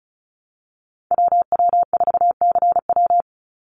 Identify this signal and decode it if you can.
Morse code, heard as a keyed tone, about 34 wpm, 700 Hz. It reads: WW4CW
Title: USA Call Signs 35wpm